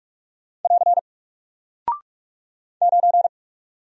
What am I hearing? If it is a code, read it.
Morse code, heard as a keyed tone, about 44 wpm, 700 Hz. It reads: <AR> 9
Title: Single Character (Letter:Number:Punctuation:Prosign) 45wpm